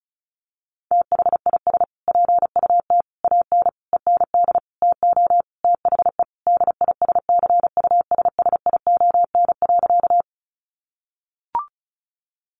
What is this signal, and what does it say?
Morse code, heard as a keyed tone, about 34 wpm, 700 Hz. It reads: THIS PUT AN END TO THE DISCUSSION.
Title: Sentences from Top 1100 Words 35wpm